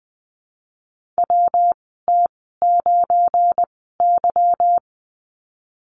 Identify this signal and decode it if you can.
Morse code, heard as a keyed tone, about 20 wpm, spaced as for 15 wpm, 700 Hz. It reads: WT9Y
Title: 186 Lesson 30g - Letter b - Call Signs 20wpm